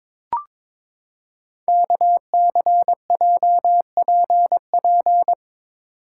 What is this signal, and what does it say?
Morse code, heard as a keyed tone, about 23 wpm, 700 Hz. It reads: KCJPP
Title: Code Groups - 5 Letters - 22wpm